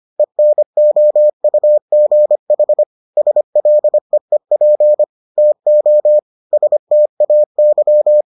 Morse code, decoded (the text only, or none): ENOUGH SLEEP TO STAY